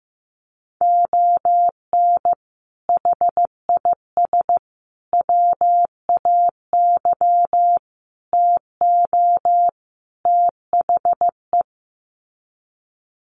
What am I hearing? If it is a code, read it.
Morse code, heard as a keyed tone, about 15 wpm, 700 Hz. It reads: ON HIS WAY TO THE
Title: Sets of 5 Words Encore2 15wpm